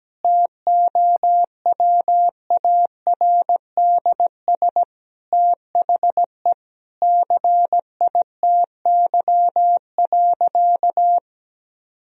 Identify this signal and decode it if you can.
Morse code, heard as a keyed tone, about 17 wpm, 700 Hz. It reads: TOWARDS THE CITY.
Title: Sentences from Top 1000 Words - Review Part 2 - 17wpm